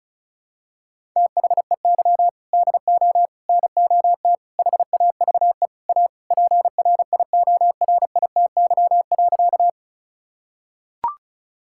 Morse code, heard as a keyed tone, about 36 wpm, 700 Hz. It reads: THEY DO NOT HAVE A PRIORITY.
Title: Sentences from Top 1600 Words 35wpm